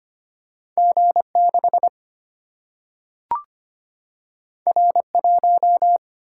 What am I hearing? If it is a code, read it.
Morse code, heard as a keyed tone, about 25 wpm, 700 Hz. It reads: G 6 R 1
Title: Single-Letter Single-Number 25wpm